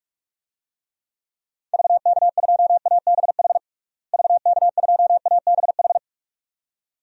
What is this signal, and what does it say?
Morse code, heard as a keyed tone, about 46 wpm, 700 Hz. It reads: VK2ABH VK2ABH
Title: DXCC-100 Call Signs - 6 Characters - Repeated 2x - 45wpm